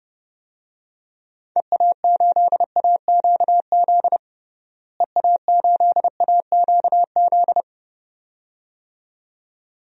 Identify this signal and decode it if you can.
Morse code, heard as a keyed tone, about 32 wpm, 700 Hz. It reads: EA8AQZ EA8AQZ
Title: DXCC-100 Call Signs - 6 Characters - Repeated 2x - 30wpm